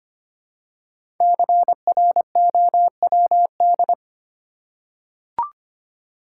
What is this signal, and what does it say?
Morse code, heard as a keyed tone, about 25 wpm, 700 Hz. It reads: CROWD
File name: 5 Letter Words - First 3 Letters Given - 25wpm